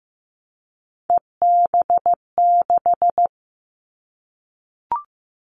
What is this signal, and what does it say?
Morse code, heard as a keyed tone, about 15 wpm, 700 Hz. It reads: EB6
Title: Callsign Prefixes for Top 50 DXCC - 15wpm